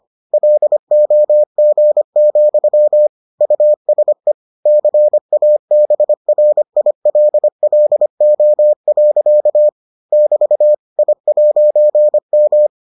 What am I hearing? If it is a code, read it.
Morse code, heard as a keyed tone, about 25 wpm, 600 Hz. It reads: LOG, USE CABRILLO. <BT> I'M